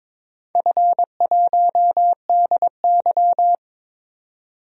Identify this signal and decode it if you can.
Morse code, heard as a keyed tone, about 23 wpm, 700 Hz. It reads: F1DY
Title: DXCC-100 Call Signs - 4 Characters - 22wpm